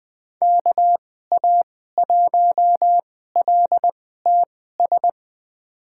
Morse code, heard as a keyed tone, about 20 wpm, spaced as for 14 wpm, 700 Hz. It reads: KA1LTS
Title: USA Call Signs - 2x Character Spacing - 20wpm